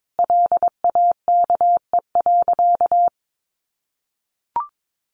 Morse code, heard as a keyed tone, about 22 wpm, 700 Hz. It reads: LAKE.
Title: Sentences from Top 1100 Words 22wpm